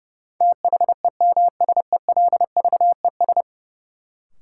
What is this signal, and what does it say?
Morse code, heard as a keyed tone, about 31 wpm, 700 Hz. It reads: THEMSELVES
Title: Sentences from Top 800 Words - New Word Drill - 30wpm